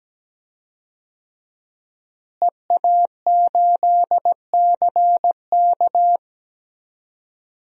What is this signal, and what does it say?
Morse code, heard as a keyed tone, about 17 wpm, 700 Hz. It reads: EA8CK
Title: DXCC-100 Call Signs - 5 Characters - Repeated 2x - 17wpm